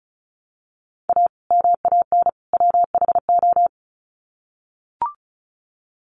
Morse code, heard as a keyed tone, about 34 wpm, 700 Hz. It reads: A MAN WHO
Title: Sets of 3 Words 35wpm